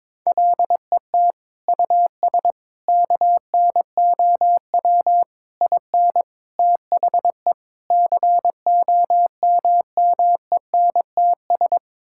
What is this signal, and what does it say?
Morse code, heard as a keyed tone, about 22 wpm, 700 Hz. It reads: LET US KNOW IN THE COMMENTS
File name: practice / 22wpm / Sets of 6 Words - 22wpm